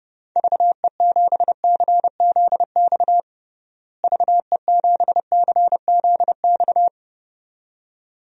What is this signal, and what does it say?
Morse code, heard as a keyed tone, about 32 wpm, 700 Hz. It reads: VE7CZX VE7CZX
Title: DXCC-100 Call Signs - 6 Characters - Repeated 2x - 30wpm